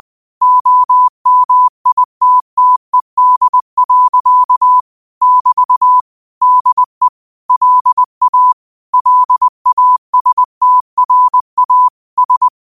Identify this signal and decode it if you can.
Morse code, heard as a keyed tone, about 20 wpm, 1000 Hz. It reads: OMITTED. <BT> DE LA LASTRA S